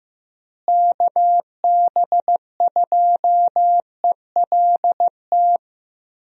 Morse code, heard as a keyed tone, about 15 wpm, 700 Hz. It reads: KB2ELT
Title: USA Call Signs - 6 Characters - Repeated 2x - 15wpm